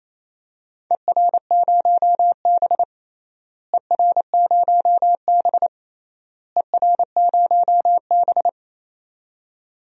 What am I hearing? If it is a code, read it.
Morse code, heard as a keyed tone, about 28 wpm, 700 Hz. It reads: ER0B ER0B ER0B
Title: DXCC-100 Call Signs - 4 Characters - Repeated 3x - 28wpm